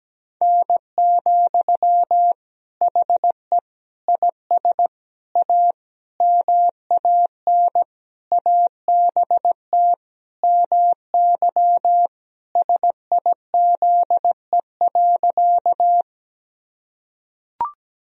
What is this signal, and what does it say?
Morse code, heard as a keyed tone, about 17 wpm, 700 Hz. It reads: N, HE IS A MAN ABT MY SIZE.
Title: Sentences from Top 400 Words and Abbreviations 17wpm